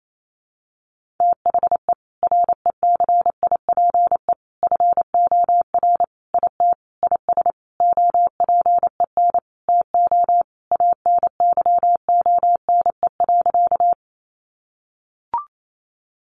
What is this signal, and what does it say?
Morse code, heard as a keyed tone, about 28 wpm, 700 Hz. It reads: THE RECIPE FOR IT IS OPEN TO ANYONE.
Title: Sentences from Top 1200 Words 28wpm